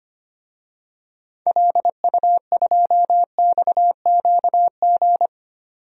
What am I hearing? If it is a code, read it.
Morse code, heard as a keyed tone, about 25 wpm, 700 Hz. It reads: LU2XQG
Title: DXCC-100 Call Signs - 6 Characters - 25wpm